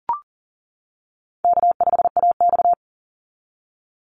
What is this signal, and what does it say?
Morse code, heard as a keyed tone, about 40 wpm, 700 Hz. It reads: K5AX
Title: Call Signs - 40wpm